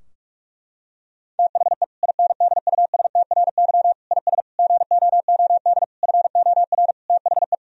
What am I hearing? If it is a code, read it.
Morse code, heard as a keyed tone, about 46 wpm, 700 Hz. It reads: THE INDUSTRY IS GOOD FOR THE
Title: Sentences from Top 1000 Words - Review Part 4 - 45wpm